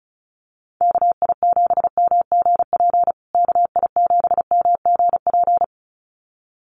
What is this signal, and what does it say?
Morse code, heard as a keyed tone, about 34 wpm, 700 Hz. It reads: KI7MGP KI7MGP
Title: USA Call Signs Repeated 2x 35wpm